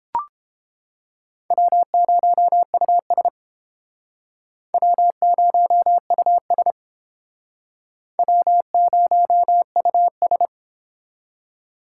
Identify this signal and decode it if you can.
Morse code, heard as a keyed tone, about 31 wpm, 700 Hz. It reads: W0US W0US W0US